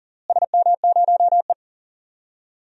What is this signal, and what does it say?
Morse code, heard as a keyed tone, about 39 wpm, 700 Hz. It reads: SM0E
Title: DXCC-100 Call Signs - 4 Characters - Repeated 2x - 40wpm